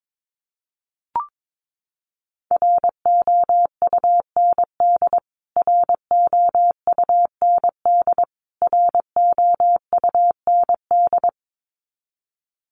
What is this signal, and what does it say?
Morse code, heard as a keyed tone, about 22 wpm, 700 Hz. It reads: ROUND ROUND ROUND
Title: Top 200 Words Repeated 22wpm